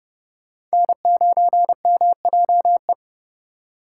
Morse code, heard as a keyed tone, about 34 wpm, 700 Hz. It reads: N9MJE
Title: Speed Racing - USA 1x3 Call Signs 30wpm